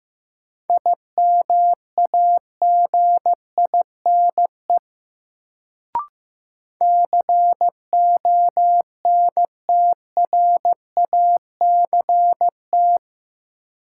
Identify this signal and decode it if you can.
Morse code, heard as a keyed tone, about 15 wpm, 700 Hz. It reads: IMAGINE CONTRACT
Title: Sentences from Top 1400 Words - New Word Drill - 15wpm